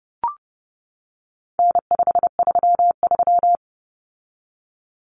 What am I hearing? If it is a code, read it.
Morse code, heard as a keyed tone, about 30 wpm, 700 Hz. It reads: N533
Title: Common Cut Numbers - 4-Digit 30wpm